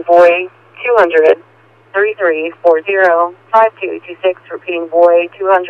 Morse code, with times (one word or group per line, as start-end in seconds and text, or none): none